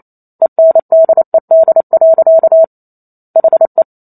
Morse code, heard as a keyed tone, about 28 wpm, 650 Hz. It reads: ENDED. HE